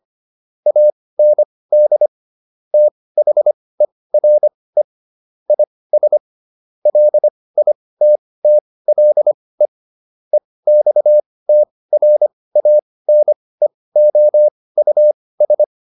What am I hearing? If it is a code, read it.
Morse code, heard as a keyed tone, about 25 wpm, spaced as for 18 wpm, 600 Hz. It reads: AND THERE IS LITTLE EXTRANEOUS